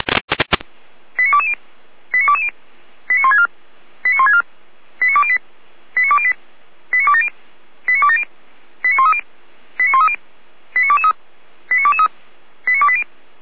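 Five-tone selective call (ZVEI), heard as quick runs of five tones, tones of about 80 ms. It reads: 89220 89220 89164 89164 89208 89208 89270 89270 89130 89130 89223 89223 89280
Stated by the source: ZVEI five-tone selective call received with an SDR.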